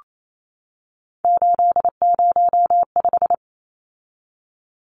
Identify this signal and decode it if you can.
Morse code, heard as a keyed tone, about 28 wpm, 700 Hz. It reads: 805
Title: Numbers - 3-Digit 28wpm